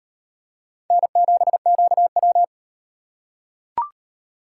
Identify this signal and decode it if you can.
Morse code, heard as a keyed tone, about 38 wpm, 700 Hz. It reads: N7QW